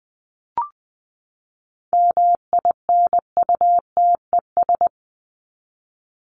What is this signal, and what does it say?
Morse code, heard as a keyed tone, about 20 wpm, 700 Hz. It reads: MINUTES
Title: ES:S Suffix - Top 100 Words with Base Words 20wpm